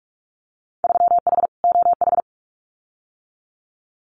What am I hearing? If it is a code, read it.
Morse code, heard as a keyed tone, about 44 wpm, 700 Hz. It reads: 3H OH
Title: Winter Field Day Exchanges Encore 45wpm